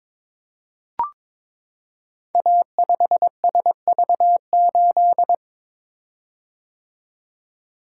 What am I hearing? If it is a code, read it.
Morse code, heard as a keyed tone, about 22 wpm, 700 Hz. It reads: A5SV8
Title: Code Groups - 5 Letter:Numbers - 22wpm